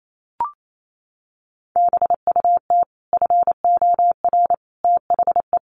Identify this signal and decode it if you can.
Morse code, heard as a keyed tone, about 28 wpm, 690 Hz. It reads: BUT FOR THE
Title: Sets of 5 Words Encore 28wpm